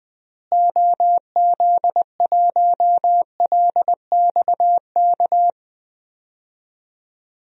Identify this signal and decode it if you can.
Morse code, heard as a keyed tone, about 20 wpm, 700 Hz. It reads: OZ1LXK
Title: DXCC-100 Call Signs - 6 Characters - Repeated 2x - 20wpm